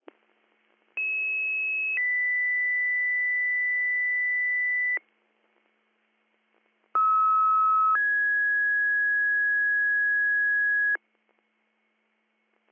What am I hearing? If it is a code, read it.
Two-tone page, heard as a one-second tone, then a three-second tone, about 2600 then 2000 Hz, about 1290 then 1650 Hz. It is heard from 1.0 to 11.0 s.